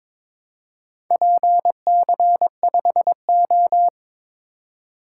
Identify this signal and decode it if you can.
Morse code, heard as a keyed tone, about 22 wpm, 700 Hz. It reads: PC5O
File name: practice / 22wpm / Four Letter-Number Combinations - 22wpm